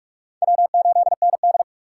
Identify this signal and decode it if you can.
Morse code, heard as a keyed tone, about 44 wpm, 690 Hz. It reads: W8ND
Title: USA Call Signs - 4 Characters - 3x Character Spacing - 45wpm